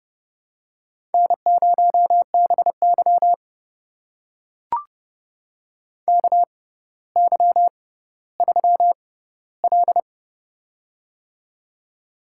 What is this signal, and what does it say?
Morse code, heard as a keyed tone, about 29 wpm, spaced as for 15 wpm, 700 Hz. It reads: N0BY KY3L
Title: USA Call Signs - 4 Characters - 6x Character Spacing - 30wpm